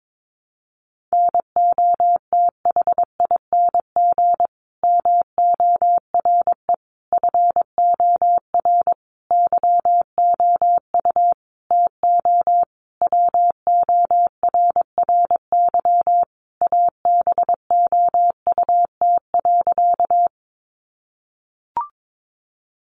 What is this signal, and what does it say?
Morse code, heard as a keyed tone, about 22 wpm, 700 Hz. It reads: NOTHING MORE FOR YOU TO WORRY ABOUT.
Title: Sentences from Top 1100 Words 22wpm